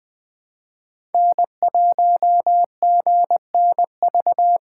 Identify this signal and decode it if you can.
Morse code, heard as a keyed tone, about 19 wpm, 700 Hz. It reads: N1GNV